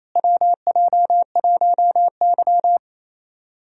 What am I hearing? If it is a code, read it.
Morse code, heard as a keyed tone, about 28 wpm, 700 Hz. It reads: WJ1Y
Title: USA Call Signs - 4 Characters - Repeated 3x - 28wpm